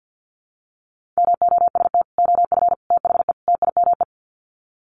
Morse code, heard as a keyed tone, about 50 wpm, 700 Hz. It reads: MOST OF THE TIME
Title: Sets of 4 Words 50wpm